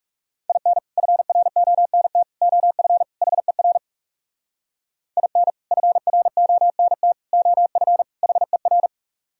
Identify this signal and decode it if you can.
Morse code, heard as a keyed tone, about 40 wpm, 700 Hz. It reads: IN FRONT OF HER IN FRONT OF HER